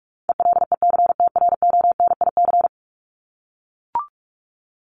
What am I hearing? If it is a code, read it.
Morse code, heard as a keyed tone, about 44 wpm, 700 Hz. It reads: ELECTRONIC
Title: Top 1001-1250 Words 45wpm